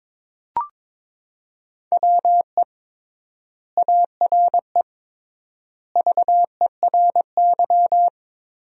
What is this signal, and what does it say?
Morse code, heard as a keyed tone, about 22 wpm, 700 Hz. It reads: WE ARE VERY
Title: 3x Word-Spacing - Sentences from Top 300 Words - 22wpm